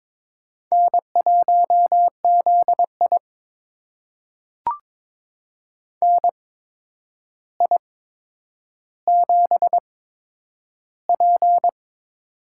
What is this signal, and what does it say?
Morse code, heard as a keyed tone, about 22 wpm, spaced as for 9 wpm, 700 Hz. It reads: N1ZI NI7P
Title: USA Call Signs - 4 Characters - 8x Character Spacing - 22wpm